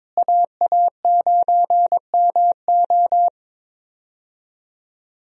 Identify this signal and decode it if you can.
Morse code, heard as a keyed tone, about 22 wpm, 700 Hz. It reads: AA9MO
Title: USA Call Signs - 5 Characters - Repeated 2x - 22wpm